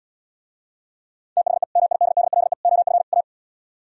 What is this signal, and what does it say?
Morse code, heard as a keyed tone, about 77 wpm, 700 Hz. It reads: THE MIDDLE OF A